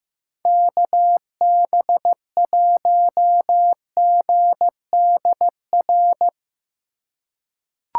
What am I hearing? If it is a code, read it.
Morse code, heard as a keyed tone, about 15 wpm, 700 Hz. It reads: KB1GDR
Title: USA Call Signs - 6 Characters - 3x Character Spacing - 15wpm